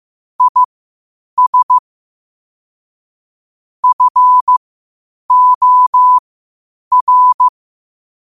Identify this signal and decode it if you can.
Morse code, heard as a keyed tone, about 15 wpm, spaced as for 8 wpm, 1000 Hz. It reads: IS FOR
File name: qotdfiles / 8wpm